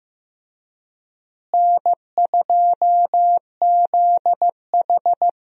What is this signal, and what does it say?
Morse code, heard as a keyed tone, about 15 wpm, 700 Hz. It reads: N2ZH
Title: USA Call Signs - 4 Characters - Repeated 3x - 15wpm